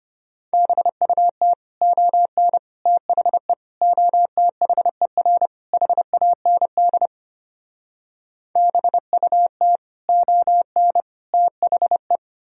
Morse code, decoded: BUT ON THE OTHER HAND BUT ON THE

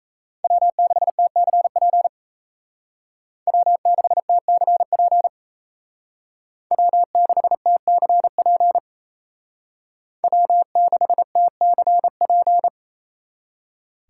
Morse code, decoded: W6TCP W6TCP W6TCP W6TCP

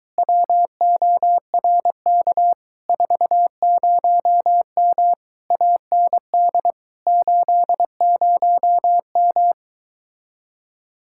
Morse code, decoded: WORK 40M AND 80M

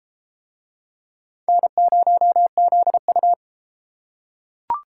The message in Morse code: N0ZU

